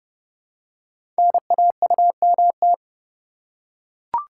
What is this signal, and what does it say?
Morse code, heard as a keyed tone, about 30 wpm, 700 Hz. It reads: NAUMT
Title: Code Groups - 5 Letters - 30wpm